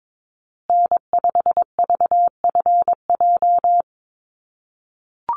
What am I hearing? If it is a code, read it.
Morse code, heard as a keyed tone, about 22 wpm, 700 Hz. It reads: N5VFJ
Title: USA Call Signs Repeated 3x 22wpm